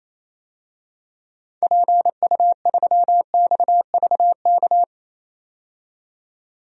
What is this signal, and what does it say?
Morse code, heard as a keyed tone, about 28 wpm, 700 Hz. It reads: PU3XVK
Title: DXCC-100 Call Signs - 6 Characters - 28wpm